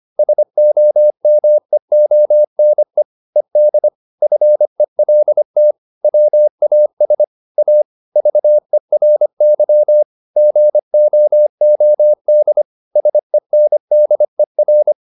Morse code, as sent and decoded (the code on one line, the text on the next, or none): ... --- -- . --- -. / . -.. / ..-. .-.. - / .-- .- ... / .- / ...- . .-. -.-- / --. --- --- -.. / ... -. -.. .-.
SOMEONE ED FELT WAS A VERY GOOD SENDER